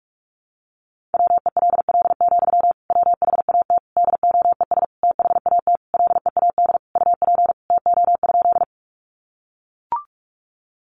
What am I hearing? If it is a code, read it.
Morse code, heard as a keyed tone, about 44 wpm, 700 Hz. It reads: WELL, WHAT DOES THAT LEAD UP TO?
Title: Sentences from Top 500 Words 45wpm